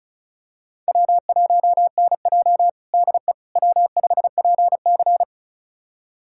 Morse code, heard as a keyed tone, about 36 wpm, 700 Hz. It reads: W1NJ DE W5PC
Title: USA Callsign DE Callsign - 4 Characters - 35wpm